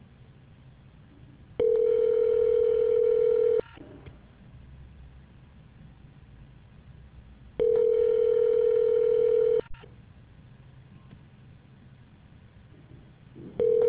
Ringback tone (North America), 440 + 480 Hz.